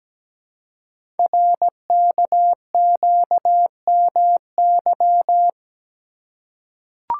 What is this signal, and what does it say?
Morse code, heard as a keyed tone, about 17 wpm, 700 Hz. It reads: RKQMY
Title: Code Groups - 5 Letter:Numbers - 17wpm